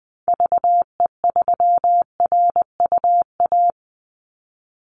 Morse code, heard as a keyed tone, about 20 wpm, 700 Hz. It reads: VE3RUA
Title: Call Signs Repeated 2x - 20wpm